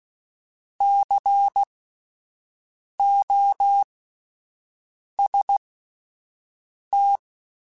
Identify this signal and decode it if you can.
Morse code, heard as a keyed tone, about 16 wpm, spaced as for 5 wpm, 800 Hz. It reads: COST
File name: Spaced Alphas / Spaced Alphas mp3s / Cs sp @10